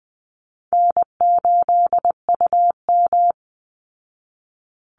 Morse code, heard as a keyed tone, about 20 wpm, 700 Hz. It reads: N8UM
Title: Easy Call Signs Repeated 2x - 20wpm